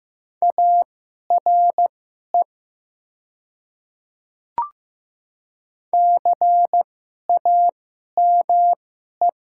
Morse code, heard as a keyed tone, about 15 wpm, spaced as for 11 wpm, 700 Hz. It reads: ARE CAME
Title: 2x Character Spacing - Top 200 Words - 15wpm